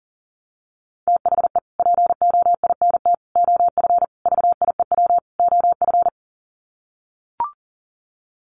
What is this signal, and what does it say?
Morse code, heard as a keyed tone, about 40 wpm, 700 Hz. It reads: THE POINT OF VIEW OF
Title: Sets of 5 Words 40wpm